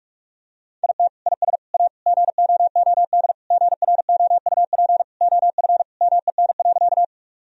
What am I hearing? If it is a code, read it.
Morse code, heard as a keyed tone, about 42 wpm, 700 Hz. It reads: IT IS A GOOD GROUP OF MEN.
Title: Sentences from Top 1000 Words - Review Part 1 - 45wpm